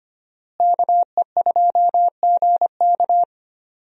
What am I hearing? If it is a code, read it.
Morse code, heard as a keyed tone, about 25 wpm, 700 Hz. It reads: KE2GK
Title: USA Call Signs - 5 Characters - Repeated 3x - 25wpm